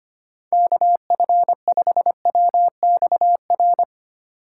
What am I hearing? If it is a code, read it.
Morse code, heard as a keyed tone, about 25 wpm, 700 Hz. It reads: KF5WXR
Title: USA Call Signs - 6 Characters - 8x Character Spacing - 25wpm